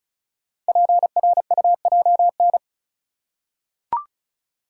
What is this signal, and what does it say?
Morse code, heard as a keyed tone, about 36 wpm, 690 Hz. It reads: PRUJN
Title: Code Groups - 5 Letters - 35wpm